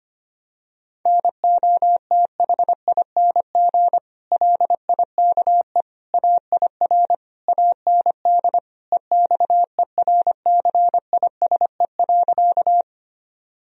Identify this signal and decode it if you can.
Morse code, heard as a keyed tone, about 25 wpm, 700 Hz. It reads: NOTHING LIKE AIR AND EXERCISE.
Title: Sentences from Top 1000 Words - Review Part 4 - 25wpm